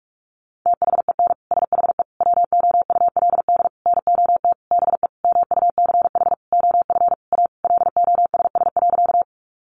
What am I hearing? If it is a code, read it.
Morse code, heard as a keyed tone, about 44 wpm, 700 Hz. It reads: THEN SHE WOULD NOT BE MUCH OF A LOSS.
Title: Sentences from Top 1400 Words 45wpm